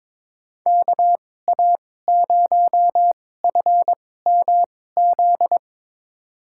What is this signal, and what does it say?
Morse code, heard as a keyed tone, about 22 wpm, spaced as for 16 wpm, 700 Hz. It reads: KA0FMZ
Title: USA Call Signs - 6 Characters - 2x Character Spacing - 22wpm